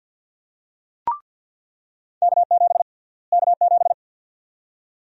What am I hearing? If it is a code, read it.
Morse code, heard as a keyed tone, about 48 wpm, 700 Hz. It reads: K7 K7
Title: USA Call Sign Prefixes - Repeated 2x - 50wpm